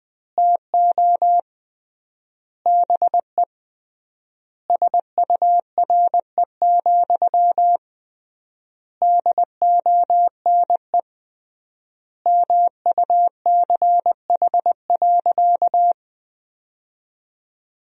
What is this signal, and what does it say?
Morse code, heard as a keyed tone, about 21 wpm, 700 Hz. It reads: TO BE SURE, DONE MUCH.
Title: 3x Word-Spacing - Sentences from Top 500 Words - 20wpm